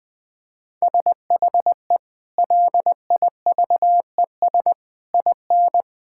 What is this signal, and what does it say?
Morse code, heard as a keyed tone, about 20 wpm, 700 Hz. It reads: SHE LIVES IN
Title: Sentences with 6-Word Phrases - 20wpm